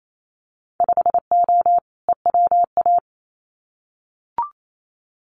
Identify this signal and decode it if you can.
Morse code, heard as a keyed tone, about 28 wpm, 700 Hz. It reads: 5O EWA
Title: Winter Field Day Exchanges Encore 28wpm